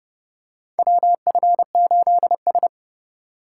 Morse code, heard as a keyed tone, about 30 wpm, 700 Hz. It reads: WF8S
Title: USA Call Signs - 4 Characters - 2x Character Spacing - 30wpm